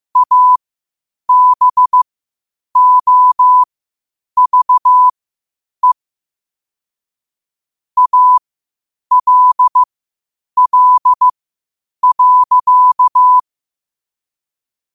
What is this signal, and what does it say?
Morse code, heard as a keyed tone, about 15 wpm, spaced as for 8 wpm, 1000 Hz. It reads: ABOVE ALL.
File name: 8wpm